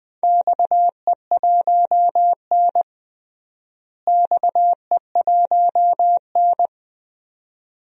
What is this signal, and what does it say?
Morse code, heard as a keyed tone, about 20 wpm, 700 Hz. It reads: XE1N XE1N
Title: DXCC-100 Call Signs - 4 Characters - Repeated 2x - 20wpm